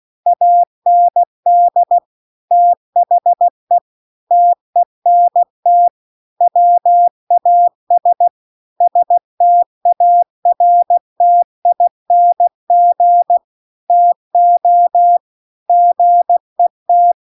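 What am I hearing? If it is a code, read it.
Morse code, heard as a keyed tone, about 16 wpm, 700 Hz. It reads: AND THE TENT WAS STARTING TO GET
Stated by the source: Morse practice text at 16 WPM.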